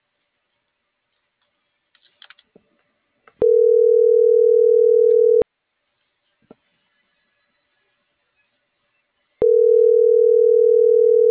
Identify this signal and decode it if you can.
Ringback tone (North America), 440 + 480 Hz.